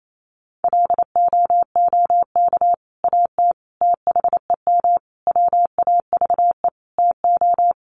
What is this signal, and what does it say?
Morse code, heard as a keyed tone, about 28 wpm, 700 Hz. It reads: LOOK AT THEM WAVE TO
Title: Sentences from Top 1400 Words 28wpm